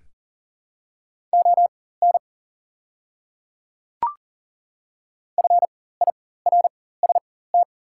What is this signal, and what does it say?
Morse code, heard as a keyed tone, about 40 wpm, spaced as for 19 wpm, 700 Hz. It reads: ON FIRST
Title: 4x Character Spacing - Top 100 Words - 40wpm